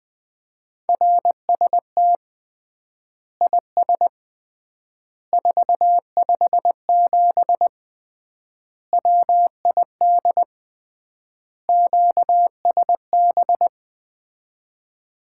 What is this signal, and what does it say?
Morse code, heard as a keyed tone, about 21 wpm, 700 Hz. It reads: RST IS 457 WID QSB
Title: Common QSO Phrases - 3x Word-Spacing - 20wpm